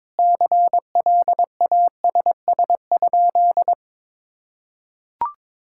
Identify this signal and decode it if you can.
Morse code, heard as a keyed tone, about 22 wpm, 700 Hz. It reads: CLASS?
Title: Sentences from Top 1000 Words - Review Part 6 - 22wpm